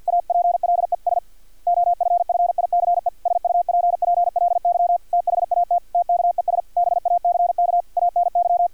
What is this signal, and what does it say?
Morse code, heard as a keyed tone, about 50 wpm, 700 Hz, with radio noise. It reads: APLES OFFICE SUPPLY THAT TKES BACK ANY
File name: archived / CRSnet